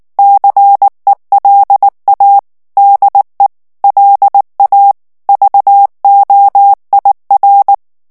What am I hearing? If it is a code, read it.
Morse code, heard as a keyed tone, about 19 wpm, 800 Hz. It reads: CELA DE LA VOIR
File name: CW / FICHIERS-MP3 / FRANCAIS TEXTE CLAIR 2 VITESSE 18 MOTS MINUTES